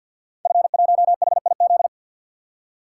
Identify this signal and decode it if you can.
Morse code, heard as a keyed tone, about 50 wpm, 700 Hz. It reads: F1HIZ
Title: Code Groups - 5 Letter:Numbers - 50wpm